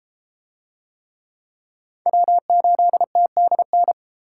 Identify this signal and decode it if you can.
Morse code, heard as a keyed tone, about 32 wpm, 700 Hz. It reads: W8TDN